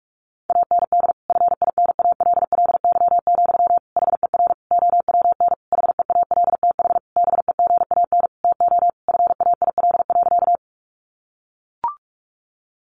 Morse code, heard as a keyed tone, about 44 wpm, 700 Hz. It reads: AND FINALLY, HER OWN HEALTH BEGAN TO FAIL.
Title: Sentences from Top 1100 Words 45wpm